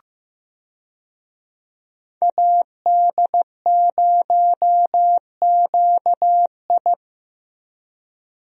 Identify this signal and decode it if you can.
Morse code, heard as a keyed tone, about 15 wpm, 700 Hz. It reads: AD0QI
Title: USA Call Signs - 5 Characters - 15wpm